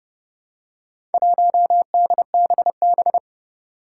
Morse code, heard as a keyed tone, about 31 wpm, 700 Hz. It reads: 1DBB
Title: Four Challenging Letter-Number Combinations 30wpm